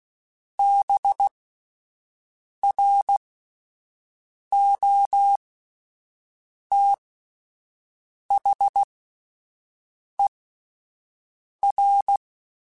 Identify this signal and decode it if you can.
Morse code, heard as a keyed tone, about 16 wpm, spaced as for 5 wpm, 800 Hz. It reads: BROTHER